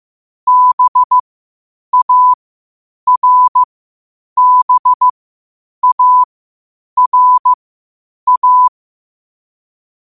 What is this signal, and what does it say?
Morse code, heard as a keyed tone, about 15 wpm, spaced as for 8 wpm, 1000 Hz. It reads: BARBARA
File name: qotdfiles / 8wpm